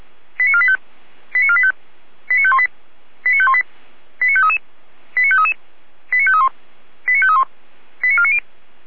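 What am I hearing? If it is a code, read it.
Five-tone selective call (ZVEI), heard as quick runs of five tones, tones of about 70 ms. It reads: 89485 89485 89518 89518 89533 89533 89531 89531 89490
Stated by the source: ZVEI five-tone selective call received by SDR.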